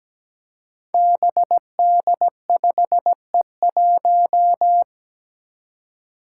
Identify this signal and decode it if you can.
Morse code, heard as a keyed tone, about 17 wpm, 700 Hz. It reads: BD5E1
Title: Five Challenging Letter-Number Combinations 17wpm